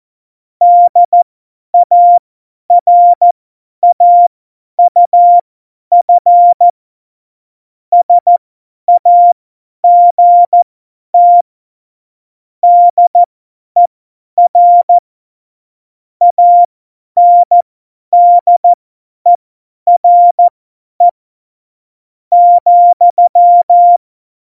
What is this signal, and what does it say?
Morse code, heard as a keyed tone, about 14 wpm, spaced as for 10 wpm, 700 Hz. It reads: DARAUF SAGT DER ANDERE ,
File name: Morsekurs / Witze / Witz 12. Juni 2018, 50 F70